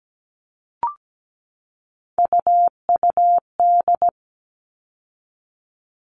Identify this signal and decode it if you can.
Morse code, heard as a keyed tone, about 17 wpm, 700 Hz. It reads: UUD
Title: All Cut Numbers - 3-Digit 17wpm